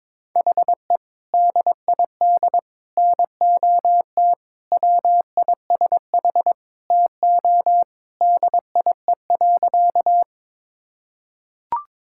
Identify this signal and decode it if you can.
Morse code, heard as a keyed tone, about 21 wpm, 700 Hz. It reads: HE DID NOT WISH TO DIE.
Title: Sentences from Top 1600 Words 22wpm